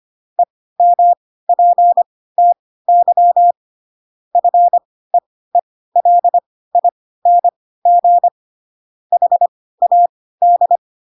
Morse code, heard as a keyed tone, about 25 wpm, spaced as for 16 wpm, 700 Hz. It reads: EMPTY FEELING HAD
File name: Pinocchio / Pinocchio ch5 10 25 ik1hge